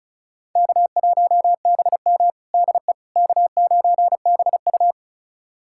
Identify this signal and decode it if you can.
Morse code, heard as a keyed tone, about 34 wpm, 700 Hz. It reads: K1BM DE K9BU
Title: USA Callsign DE Callsign - 4 Characters - 35wpm